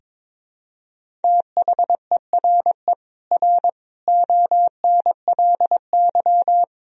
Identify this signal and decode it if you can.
Morse code, heard as a keyed tone, about 22 wpm, 700 Hz. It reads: THERE R ONLY
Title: Sentences from Top 200 Words and Abbreviations 22wpm